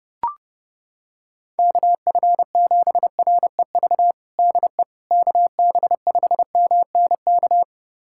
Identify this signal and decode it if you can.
Morse code, heard as a keyed tone, about 30 wpm, 700 Hz. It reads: KF7REV DE KB5MNK
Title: USA Callsign DE Callsign - 6 Characters - 30wpm